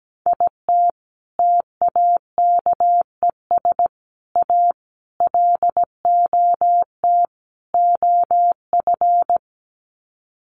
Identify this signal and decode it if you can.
Morse code, heard as a keyed tone, about 17 wpm, 700 Hz. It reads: IT TAKES A LOT OF
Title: Sets of 5 Words Encore 17wpm